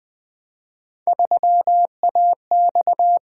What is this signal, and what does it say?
Morse code, heard as a keyed tone, about 20 wpm, 700 Hz. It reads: 3AX
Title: Three Letter-Number Combinations - 20wpm